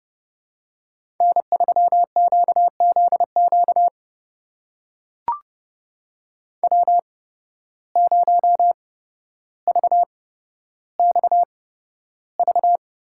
Morse code, heard as a keyed tone, about 31 wpm, spaced as for 13 wpm, 700 Hz. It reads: N3QZQ W0VXV
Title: USA Call Signs - 5 Characters - 8x Character Spacing - 30wpm